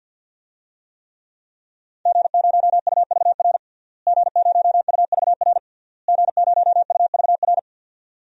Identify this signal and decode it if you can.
Morse code, heard as a keyed tone, about 47 wpm, 700 Hz. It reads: G0UVR G0UVR G0UVR
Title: DXCC-100 Call Signs - 5 Characters - Repeated 3x - 50wpm